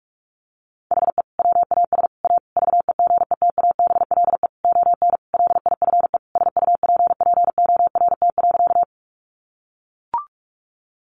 Morse code, heard as a keyed tone, about 44 wpm, 700 Hz. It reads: HE WAS A VEGETABLE ON LIFE SUPPORT.
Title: Sentences from Top 1300 Words 45wpm